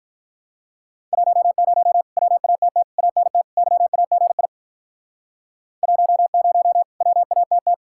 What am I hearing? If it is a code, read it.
Morse code, heard as a keyed tone, about 47 wpm, 700 Hz. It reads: 10 WATT ANT YAGI 10 WATT